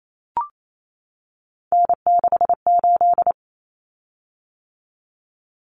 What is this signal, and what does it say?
Morse code, heard as a keyed tone, about 28 wpm, 700 Hz. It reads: N68
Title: Common Cut Numbers - 3-Digit 28wpm